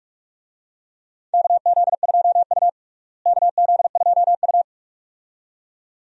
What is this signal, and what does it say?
Morse code, heard as a keyed tone, about 44 wpm, 700 Hz. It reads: KZ2U KZ2U
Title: USA Call Signs - 4 Characters - Repeated 2x - 45wpm